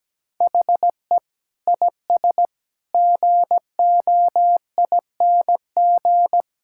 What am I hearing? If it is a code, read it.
Morse code, heard as a keyed tone, about 17 wpm, 700 Hz. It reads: HE IS GOING
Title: Sentences from Top 1600 Words 17wpm